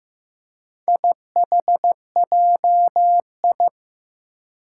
Morse code, heard as a keyed tone, about 15 wpm, 700 Hz. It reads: IHJI
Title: Four Challenging Letter-Number Combinations 15wpm